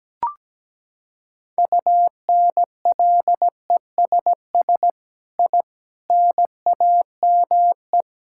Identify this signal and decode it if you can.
Morse code, heard as a keyed tone, about 17 wpm, 700 Hz. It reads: UNLESS I NAME
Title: Sentences from Top 1000 Words - Review Part 6 - 17wpm